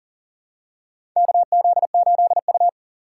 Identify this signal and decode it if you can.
Morse code, heard as a keyed tone, about 40 wpm, 700 Hz. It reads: KZ8U
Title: USA Call Signs - 4 Characters - 3x Character Spacing - 40wpm